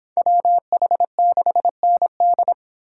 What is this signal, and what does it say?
Morse code, heard as a keyed tone, about 24 wpm, 700 Hz. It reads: WH6ND